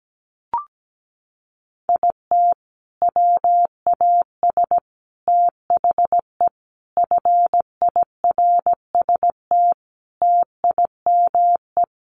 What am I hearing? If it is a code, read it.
Morse code, heard as a keyed tone, about 17 wpm, 700 Hz. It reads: IT WAS THE FIRST TIME
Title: Sets of 5 Words 17wpm